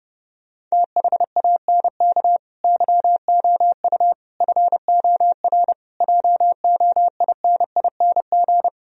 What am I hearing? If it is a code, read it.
Morse code, heard as a keyed tone, about 33 wpm, 700 Hz. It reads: THANK YOU FOR JOINING